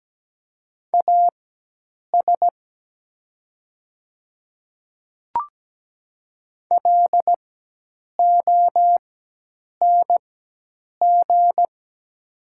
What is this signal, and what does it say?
Morse code, heard as a keyed tone, about 17 wpm, spaced as for 8 wpm, 700 Hz. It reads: AS LONG
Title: 4x Character Spacing - Top 100 Words - 17wpm